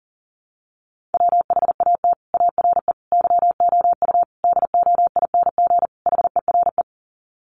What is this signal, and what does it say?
Morse code, heard as a keyed tone, about 40 wpm, 700 Hz. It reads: WHAT ARE YOU DOING HERE
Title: Sets of 5 Words 40wpm